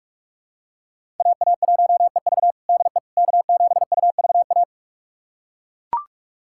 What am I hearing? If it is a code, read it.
Morse code, heard as a keyed tone, about 41 wpm, 700 Hz. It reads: AA1EV DE K7UVA
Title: USA Callsign DE Callsign - 5 Characters - 45wpm